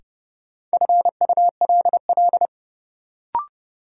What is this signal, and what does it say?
Morse code, heard as a keyed tone, about 30 wpm, 700 Hz. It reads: FULL
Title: Sentences from Top 500 Words - New Word Drill - 30wpm